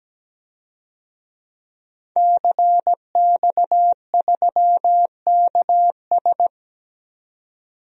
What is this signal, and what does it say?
Morse code, heard as a keyed tone, about 17 wpm, 700 Hz. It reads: CX3KS
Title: DXCC-100 Call Signs - 5 Characters - 17wpm